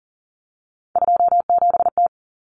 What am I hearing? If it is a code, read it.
Morse code, heard as a keyed tone, about 40 wpm, 700 Hz. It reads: 27T
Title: Common Cut Numbers - 3-Digit 40wpm